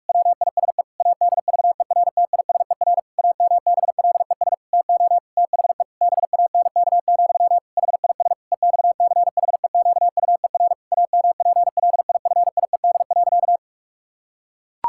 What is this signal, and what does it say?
Morse code, heard as a keyed tone, about 46 wpm, 700 Hz. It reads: WISE ADVERTISER AMBLES TO THE BANK, HIS EXCHEQUER AMPLIFIED.
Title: Pangram Challenge v2 - 45wpm